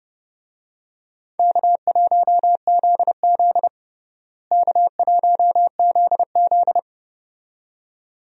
Morse code, heard as a keyed tone, about 30 wpm, 700 Hz. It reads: K1ZZ K1ZZ
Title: USA Call Signs - 4 Characters - Repeated 2x - 30wpm